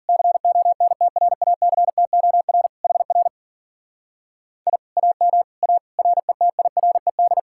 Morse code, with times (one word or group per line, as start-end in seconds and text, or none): 0.1-2.7 CONTRACTOR
2.8-3.3 HR
4.7-4.8 I
5.0-5.4 AM
5.6-5.8 A
6.0-7.4 RETIRED